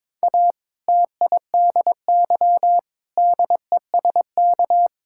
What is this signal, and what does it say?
Morse code, heard as a keyed tone, about 22 wpm, 700 Hz. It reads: A TIDY DESK
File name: Difficult Sentences with Hints - 22wpm